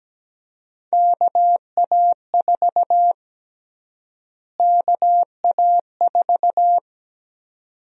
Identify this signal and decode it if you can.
Morse code, heard as a keyed tone, about 17 wpm, 700 Hz. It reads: KA4 KA4
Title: USA Call Sign Prefixes - Repeated 2x - 17wpm